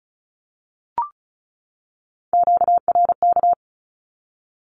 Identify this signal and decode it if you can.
Morse code, heard as a keyed tone, about 34 wpm, 700 Hz. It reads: QRK
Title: Q-Codes 35wpm